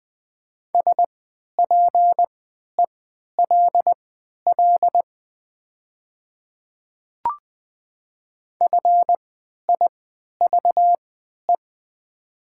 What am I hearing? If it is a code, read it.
Morse code, heard as a keyed tone, about 20 wpm, spaced as for 11 wpm, 700 Hz. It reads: SPELL FIVE
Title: 3x Character Spacing - Top 500 Words - 20wpm